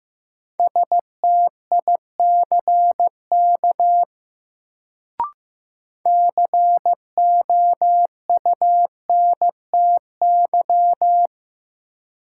Morse code, heard as a keyed tone, about 15 wpm, 700 Hz. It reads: STICK COUNTY
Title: Sentences from Top 1200 Words - New Word Drill - 15wpm